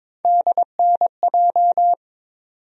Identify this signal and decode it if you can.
Morse code, heard as a keyed tone, about 22 wpm, 700 Hz. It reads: DNJ
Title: USA Call Sign Postfixes - 3 Letters 22wpm